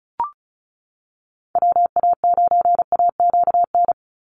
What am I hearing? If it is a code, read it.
Morse code, heard as a keyed tone, about 36 wpm, 700 Hz. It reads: WA9AQN
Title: Call Signs - 35wpm